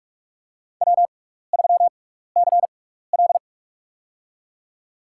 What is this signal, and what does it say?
Morse code, heard as a keyed tone, about 46 wpm, spaced as for 15 wpm, 700 Hz. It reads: W3CL
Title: USA Call Signs - 4 Characters - 6x Character Spacing - 45wpm